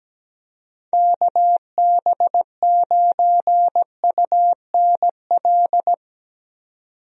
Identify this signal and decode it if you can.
Morse code, heard as a keyed tone, about 17 wpm, 700 Hz. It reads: KB9UNL
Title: USA Call Signs - 6 Characters - 17wpm